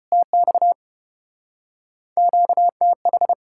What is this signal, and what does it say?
Morse code, heard as a keyed tone, about 33 wpm, 700 Hz. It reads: TX QTH